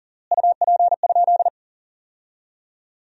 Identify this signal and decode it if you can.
Morse code, heard as a keyed tone, about 40 wpm, 700 Hz. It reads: UP?
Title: 3x Word-Spacing - Sentences from Top 500 Words - 40wpm